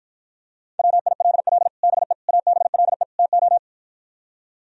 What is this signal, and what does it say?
Morse code, heard as a keyed tone, about 53 wpm, 700 Hz. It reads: WILL BE ABLE TO